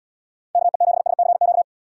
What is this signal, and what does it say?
Morse code, heard as a keyed tone, about 70 wpm, 700 Hz. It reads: KE7SCQ